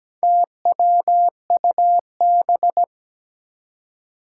Morse code, heard as a keyed tone, about 17 wpm, 700 Hz. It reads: TWUB
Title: Four Challenging Letter-Number Combinations 17wpm